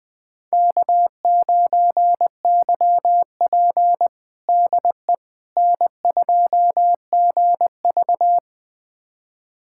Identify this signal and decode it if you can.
Morse code, heard as a keyed tone, about 20 wpm, 700 Hz. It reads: K9YP DE N2GV
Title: USA Callsign DE Callsign - 4 Characters - 20wpm